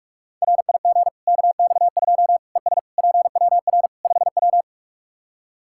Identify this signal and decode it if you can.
Morse code, heard as a keyed tone, about 46 wpm, 700 Hz. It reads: RIG KX2 ES PWR 5W
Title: K1USN SST QSO Practice 45wpm